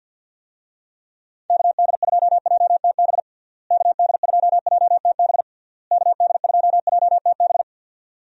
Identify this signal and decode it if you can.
Morse code, heard as a keyed tone, about 52 wpm, 700 Hz. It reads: KD2JTB KD2JTB KD2JTB
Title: USA Call Signs - 6 Characters - Repeated 3x - 50wpm